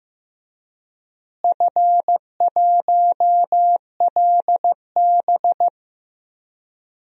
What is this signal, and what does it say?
Morse code, heard as a keyed tone, about 15 wpm, 700 Hz. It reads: F1LB
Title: DXCC-100 Call Signs - 4 Characters - Repeated 2x - 15wpm